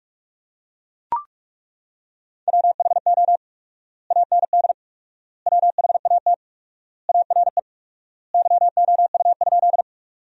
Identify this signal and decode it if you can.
Morse code, heard as a keyed tone, about 42 wpm, 700 Hz. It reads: WHO AND WHAT ARE YOU?
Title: 4x Word-Spacing - Sentences from Top 100 Words - 45wpm